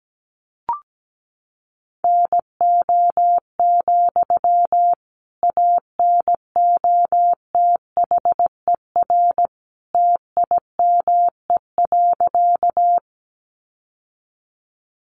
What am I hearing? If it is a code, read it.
Morse code, heard as a keyed tone, about 17 wpm, 700 Hz. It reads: NO, ANOTHER TIME.
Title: Sentences from Top 600 Words 17wpm